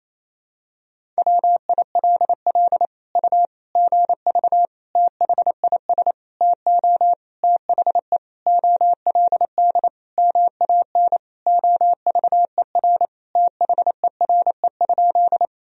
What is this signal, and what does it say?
Morse code, heard as a keyed tone, about 28 wpm, 700 Hz. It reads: WILL U GV THIS TO THE OLD MAN OVER THERE?
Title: Sentences from Top 200 Words and Abbreviations 28wpm